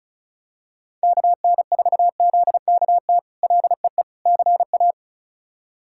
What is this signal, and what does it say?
Morse code, heard as a keyed tone, about 34 wpm, 700 Hz. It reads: KN4ZKT LEE CA
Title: K1USN SST QSO Practice - 35wpm